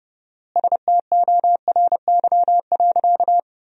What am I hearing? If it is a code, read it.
Morse code, heard as a keyed tone, about 30 wpm, 700 Hz. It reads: STORY.
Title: 4x Word-Spacing - Sentences from Top 300 Words - 30wpm